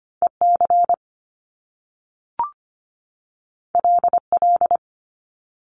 Morse code, heard as a keyed tone, about 25 wpm, 700 Hz. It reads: EC LL
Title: Top Two Letter Combinations in Words 25wpm